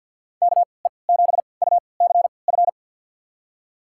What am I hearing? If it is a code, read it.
Morse code, heard as a keyed tone, about 47 wpm, spaced as for 28 wpm, 700 Hz. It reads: KE7UCF
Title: USA Call Signs - 6 Characters - 3x Character Spacing - 50wpm